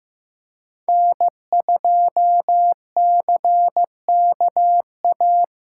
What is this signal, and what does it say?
Morse code, heard as a keyed tone, about 18 wpm, 700 Hz. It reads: N2CKA
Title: Speed Racing - USA Call Signs 15wpm